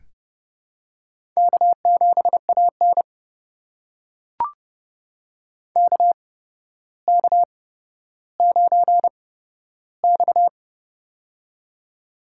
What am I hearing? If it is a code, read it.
Morse code, heard as a keyed tone, about 30 wpm, spaced as for 13 wpm, 700 Hz. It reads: K7AN KK9X
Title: USA Call Signs - 4 Characters - 8x Character Spacing - 30wpm